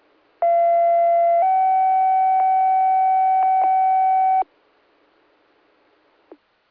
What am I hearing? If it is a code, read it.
Two-tone page, heard as a one-second tone, then a three-second tone, about 680 then 760 Hz. It is heard from 0.4 to 4.4 s.